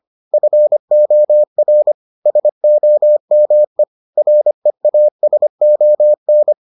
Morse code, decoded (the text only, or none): FOR SOME REASON